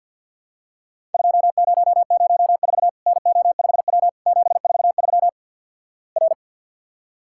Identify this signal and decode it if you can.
Morse code, heard as a keyed tone, about 50 wpm, 700 Hz. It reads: 2004 NO5W 743 R
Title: Worked All Europe DX Contest - QTC Exchange - 50wpm